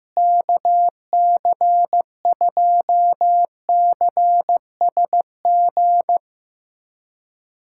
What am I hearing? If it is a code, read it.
Morse code, heard as a keyed tone, about 15 wpm, 700 Hz. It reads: KC2CSG
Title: USA Callsign DE Callsign 15wpm